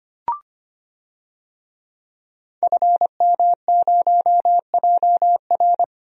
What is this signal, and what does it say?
Morse code, heard as a keyed tone, about 25 wpm, 700 Hz. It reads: FM0JR
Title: DXCC-100 Call Signs - 5 Characters - 25wpm